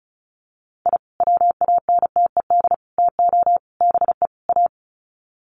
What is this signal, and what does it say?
Morse code, heard as a keyed tone, about 34 wpm, 700 Hz. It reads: I WANTED TO BE A
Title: Sets of 5 Words Encore2 35wpm